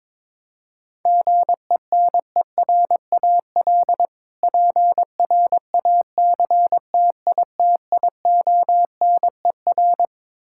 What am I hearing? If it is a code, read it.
Morse code, heard as a keyed tone, about 21 wpm, 700 Hz. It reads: GENERAL PRACTITIONER
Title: Practice QSO - Exchange 3 - More Info - 22wpm